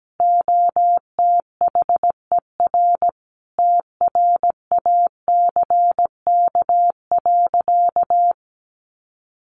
Morse code, heard as a keyed tone, about 17 wpm, 700 Hz. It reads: OTHER TRACK.
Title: Sentences from Top 1100 Words 17wpm